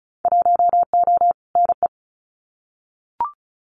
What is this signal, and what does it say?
Morse code, heard as a keyed tone, about 34 wpm, 700 Hz. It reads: 1O NE
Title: Winter Field Day Exchanges Encore 35wpm